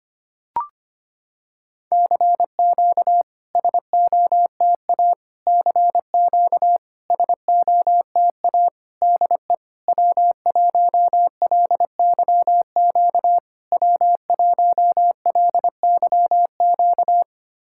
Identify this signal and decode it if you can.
Morse code, heard as a keyed tone, about 25 wpm, 700 Hz. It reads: CQ SOTA CQ SOTA DE W1LYQ W1LYQ
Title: SOTA QSO Practice 25wpm